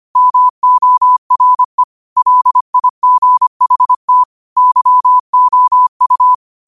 Morse code, heard as a keyed tone, about 25 wpm, 1000 Hz. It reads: MORE LIGHT YOU
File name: qotdfiles / 25wpm